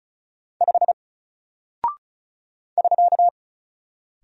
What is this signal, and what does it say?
Morse code, heard as a keyed tone, about 36 wpm, 700 Hz. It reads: 5 <SK>
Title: Single Character (Letter:Number:Punctuation:Prosign) Alt 35wpm